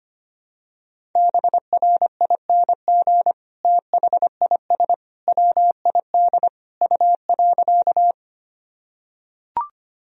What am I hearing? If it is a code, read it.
Morse code, heard as a keyed tone, about 25 wpm, 700 Hz. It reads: BRING THIS WID U.
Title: Sentences from Top 500 Words and Abbreviations 25wpm